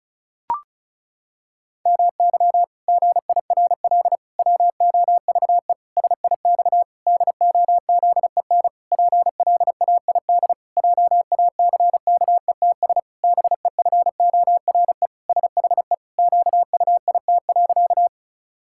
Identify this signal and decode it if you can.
Morse code, heard as a keyed tone, about 36 wpm, 700 Hz. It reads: MY GIRL WOVE SIX DOZEN PLAID JACKETS BEFORE SHE QUIT.
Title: Pangram Challenge v2 - 35wpm